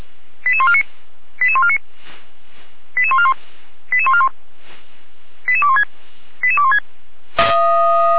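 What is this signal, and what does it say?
Five-tone selective call (ZVEI), heard as quick runs of five tones, tones of about 70 ms. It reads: 88149 88149 88141 88141 88216 88216
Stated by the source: ZVEI five-tone selective call received on an SDR.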